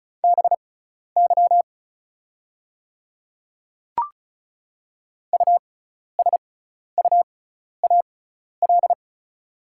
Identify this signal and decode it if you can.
Morse code, heard as a keyed tone, about 36 wpm, spaced as for 12 wpm, 700 Hz. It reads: BY USUAL
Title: 6x Character Spacing - Top 500 Words - 35wpm